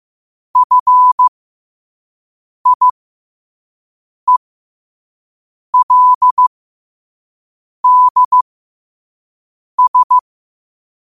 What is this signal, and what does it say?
Morse code, heard as a keyed tone, about 15 wpm, spaced as for 5 wpm, 1000 Hz. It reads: FIELDS
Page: New quotes every day in morse code at 5 Words per minute.